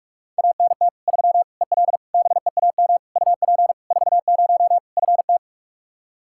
Morse code, heard as a keyed tone, about 46 wpm, 700 Hz. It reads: ANT 3 EL BEAM UP 40 FT
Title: Practice QSO - Exchange 2 - Info - 45wpm